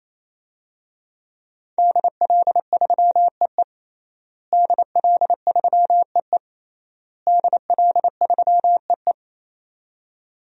Morse code, heard as a keyed tone, about 28 wpm, 690 Hz. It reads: DL3EE DL3EE DL3EE
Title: DXCC-100 Call Signs - 5 Characters - Repeated 3x - 28wpm